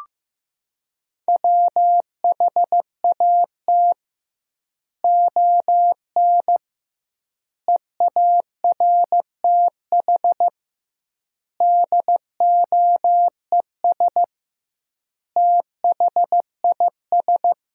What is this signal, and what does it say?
Morse code, heard as a keyed tone, about 15 wpm, 700 Hz. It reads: WHAT ON EARTH DOES THIS
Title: 2x Word-Spacing - Sentences from Top 200 Words - 15wpm